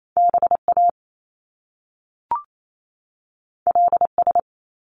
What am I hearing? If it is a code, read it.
Morse code, heard as a keyed tone, about 28 wpm, 700 Hz. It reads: B A L S
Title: Top Two Letter Combinations in Words 28wpm